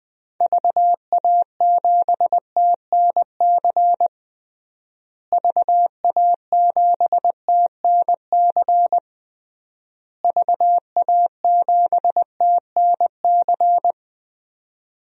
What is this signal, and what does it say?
Morse code, heard as a keyed tone, about 20 wpm, 700 Hz. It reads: VA7TNC VA7TNC VA7TNC
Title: DXCC-100 Call Signs - 6 Characters - Repeated 3x - 20wpm